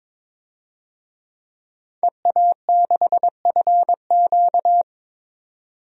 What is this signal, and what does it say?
Morse code, heard as a keyed tone, about 22 wpm, 700 Hz. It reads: EA6FQ
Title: DXCC-100 Call Signs - 5 Characters - 22wpm